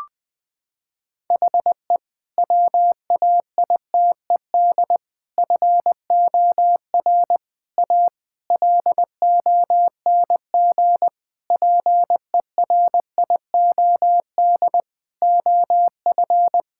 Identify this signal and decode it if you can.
Morse code, heard as a keyed tone, about 20 wpm, 700 Hz. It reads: HE WAITED FOR A LONG PERIOD OF
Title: Sentences with 5-Word Phrases - 20wpm